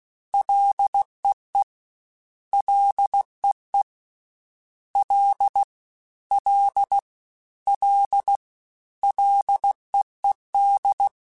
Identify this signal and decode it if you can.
Morse code, heard as a keyed tone, about 16 wpm, 800 Hz. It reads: LEE LEE L L L LEED
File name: Ls / Single L m3s / Single L Prefixe 3x2 Twice @15